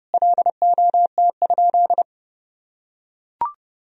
Morse code, heard as a keyed tone, about 32 wpm, 700 Hz. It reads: LOT?
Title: Sentences from Top 1000 Words - Review Part 4 - 30wpm